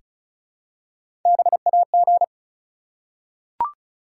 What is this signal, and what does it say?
Morse code, heard as a keyed tone, about 36 wpm, 700 Hz. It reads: BAG
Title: Three Letter Words - 35wpm